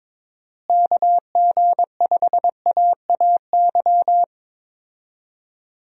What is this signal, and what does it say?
Morse code, heard as a keyed tone, about 22 wpm, 700 Hz. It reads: KG5AAY
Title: USA Call Signs - 6 Characters - Repeated 3x - 22wpm